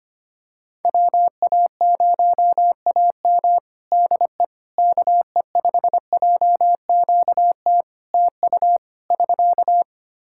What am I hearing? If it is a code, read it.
Morse code, heard as a keyed tone, about 25 wpm, 700 Hz. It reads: WA0AM DE KE5JQT TU <SK>
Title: Practice QSO - Exchange 4 - Ending - 25wpm